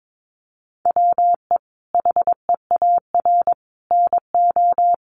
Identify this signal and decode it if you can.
Morse code, heard as a keyed tone, about 22 wpm, 700 Hz. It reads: WE HEAR NO
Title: Sentences from Top 1400 Words 22wpm